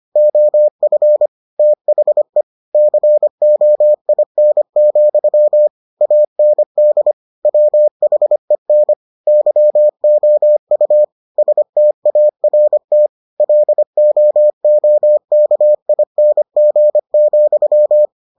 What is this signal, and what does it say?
Morse code, heard as a keyed tone, about 25 wpm, 600 Hz. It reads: OF THE COIN, AND WHEN YOU START LOOKING,